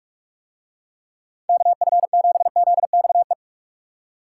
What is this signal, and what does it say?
Morse code, heard as a keyed tone, about 46 wpm, 700 Hz. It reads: KF7ZXE